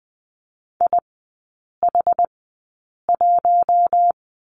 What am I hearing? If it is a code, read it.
Morse code, heard as a keyed tone, about 20 wpm, spaced as for 8 wpm, 690 Hz. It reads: IH1
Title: 081 Lesson 14e - Number 2 - Three Characters 20wpm